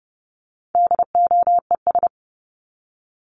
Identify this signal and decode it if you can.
Morse code, heard as a keyed tone, about 30 wpm, 700 Hz. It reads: DOES
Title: ES:S Suffix - Top 100 Words with Base Words 30wpm